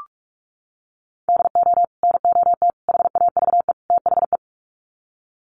Morse code, heard as a keyed tone, about 44 wpm, 700 Hz. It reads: DO NOT HAVE THE
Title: Sets of 4 Words 45wpm